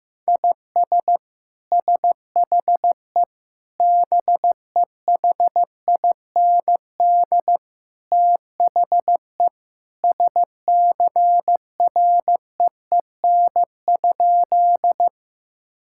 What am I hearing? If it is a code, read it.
Morse code, heard as a keyed tone, about 15 wpm, 700 Hz. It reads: IS SHE BEHIND THE SCREEN?
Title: Sentences from Top 1000 Words - Review Part 6 - 15wpm